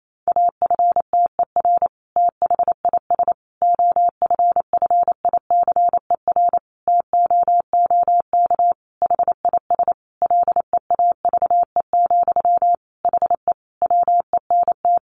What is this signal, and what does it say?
Morse code, heard as a keyed tone, about 28 wpm, 700 Hz. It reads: AFTER THIS OFFICER TOOK HIS LEAVE, HE WENT
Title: Sentences from Top 1200 Words 28wpm